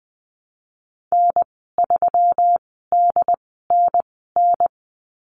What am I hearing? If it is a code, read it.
Morse code, heard as a keyed tone, about 20 wpm, spaced as for 15 wpm, 700 Hz. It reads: N3DNN
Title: 122 Lesson 20g - Number 3 - Call Signs 20wpm